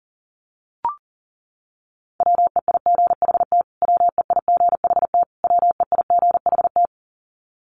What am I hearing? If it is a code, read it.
Morse code, heard as a keyed tone, about 40 wpm, 700 Hz. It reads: WEIGHT WEIGHT WEIGHT
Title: Top 500 Words Repeated 40wpm